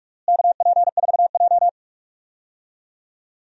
Morse code, heard as a keyed tone, about 47 wpm, 700 Hz. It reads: KP4J
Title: DXCC-100 Call Signs - 4 Characters - Repeated 2x - 45wpm